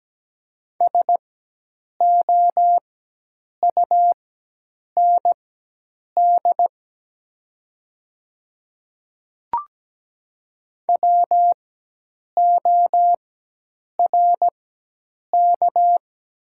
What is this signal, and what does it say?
Morse code, heard as a keyed tone, about 17 wpm, spaced as for 8 wpm, 700 Hz. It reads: SOUND WORK
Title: 4x Character Spacing - Top 300 Words - 17wpm